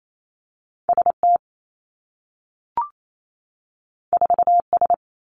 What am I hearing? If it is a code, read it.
Morse code, heard as a keyed tone, about 28 wpm, 700 Hz. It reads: S T 4 S
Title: Two Challenging Letter-Number Combinations 28wpm